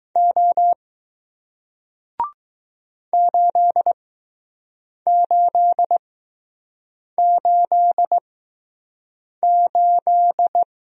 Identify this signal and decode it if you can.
Morse code, heard as a keyed tone, about 21 wpm, spaced as for 6 wpm, 700 Hz. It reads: O 8888